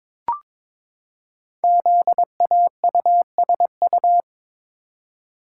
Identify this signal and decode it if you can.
Morse code, heard as a keyed tone, about 22 wpm, 700 Hz. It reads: ZAUSU
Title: Code Groups - 5 Letters - 22wpm